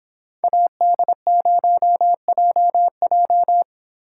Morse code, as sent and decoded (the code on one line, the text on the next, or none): .- -.. ----- .--- .---
AD0JJ